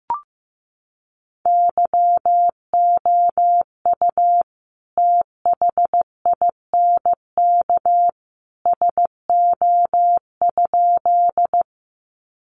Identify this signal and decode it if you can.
Morse code, heard as a keyed tone, about 15 wpm, 700 Hz. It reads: YOU THINK SO?
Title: Sentences from Top 200 Words 15wpm